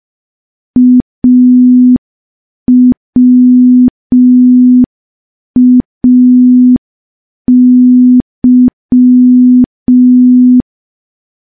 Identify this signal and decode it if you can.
Morse code, heard as a keyed tone, about 5 wpm, 250 Hz. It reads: AWAY
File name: morse / morse-training-05WPM